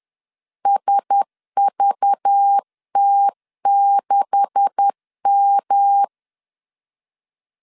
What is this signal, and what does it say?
Morse code, heard as a keyed tone, about 10 wpm, 800 Hz. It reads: SVT6M
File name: CW / DET / 1 9 1500C S C 10wpm 4557